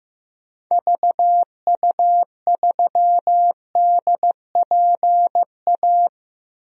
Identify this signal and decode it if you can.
Morse code, heard as a keyed tone, about 15 wpm, 700 Hz. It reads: VU3DPA
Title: DXCC-100 Call Signs - 6 Characters - 15wpm